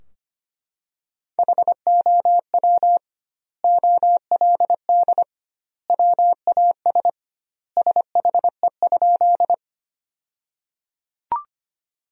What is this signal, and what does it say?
Morse code, heard as a keyed tone, about 25 wpm, 700 Hz. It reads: HOW OLD WAS SHE?
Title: 2x Word-Spacing - Sentences from Top 200 Words - 25wpm